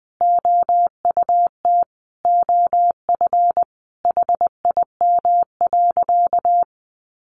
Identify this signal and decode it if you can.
Morse code, heard as a keyed tone, about 20 wpm, 700 Hz. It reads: OUT OF HIM.
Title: Sentences from Top 1500 Words 20wpm